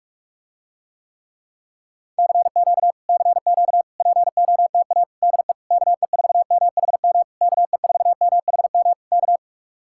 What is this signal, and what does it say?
Morse code, heard as a keyed tone, about 49 wpm, 700 Hz. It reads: CQ CQ POTA DE KE4MHM KE4MHM K